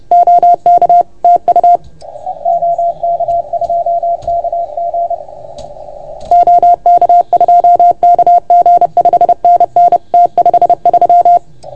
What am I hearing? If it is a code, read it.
Morse code, heard as a keyed tone, about 30 wpm, 690 Hz. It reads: OK TU OK2KG OK2KG 5NN T53
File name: Archiv / CUC381 / OK1FGD CUC 381